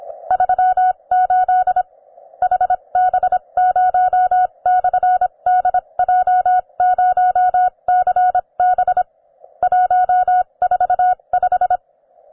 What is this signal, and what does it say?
Morse code, heard as a keyed tone, about 26 wpm, spaced as for 22 wpm, 730 Hz. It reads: 38 HB0/DJ0CB 145